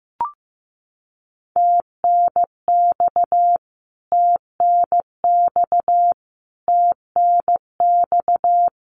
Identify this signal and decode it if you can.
Morse code, heard as a keyed tone, about 15 wpm, 700 Hz. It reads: TNX TNX TNX
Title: Abbreviations Repeated 3x - 15wpm